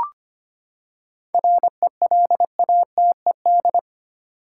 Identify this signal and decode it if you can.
Morse code, heard as a keyed tone, about 25 wpm, 700 Hz. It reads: RELATED
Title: Seven Letter Words - 25wpm